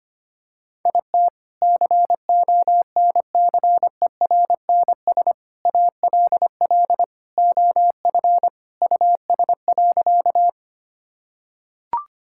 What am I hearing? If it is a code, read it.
Morse code, heard as a keyed tone, about 25 wpm, 700 Hz. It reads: IT CONCERNS ALL OF US.
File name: practice / 25wpm / Sentences with 3-Word Phrases - 25wpm